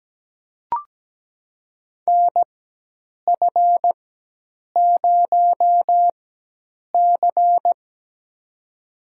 Morse code, heard as a keyed tone, about 17 wpm, spaced as for 8 wpm, 700 Hz. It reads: NF0C
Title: USA Call Signs - 4 Characters - 4x Character Spacing - 17wpm